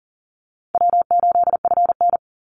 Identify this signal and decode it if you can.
Morse code, heard as a keyed tone, about 40 wpm, 700 Hz. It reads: W8FN
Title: Easy Call Signs Repeated 2x - 40wpm